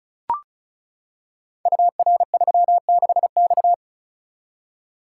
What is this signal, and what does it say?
Morse code, heard as a keyed tone, about 36 wpm, 700 Hz. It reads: UR36X
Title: Code Groups - 5 Letter:Numbers - 35wpm